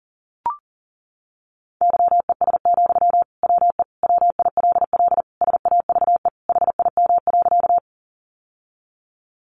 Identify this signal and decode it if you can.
Morse code, heard as a keyed tone, about 40 wpm, 700 Hz. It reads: YES, WE WILL SAVE HIM.
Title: Sentences from Top 1100 Words 40wpm